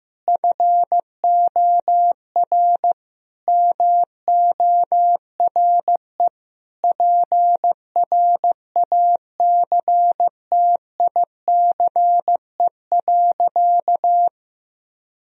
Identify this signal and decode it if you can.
Morse code, heard as a keyed tone, about 15 wpm, 700 Hz. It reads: FOR MORE PRACTICE.
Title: Sentences from Top 1000 Words - Review Part 4 - 15wpm